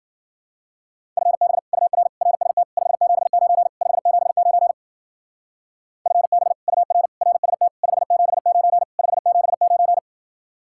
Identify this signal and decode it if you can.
Morse code, heard as a keyed tone, about 49 wpm, 700 Hz. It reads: FB UR RST 579 579 FB UR RST 579 579
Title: Speed-Racing Common QSO Phrases - 40wpm